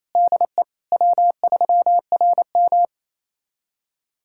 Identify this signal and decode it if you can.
Morse code, heard as a keyed tone, about 28 wpm, 700 Hz. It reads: DE W3RM
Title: USA Callsign DE Callsign - 4 Characters - 28wpm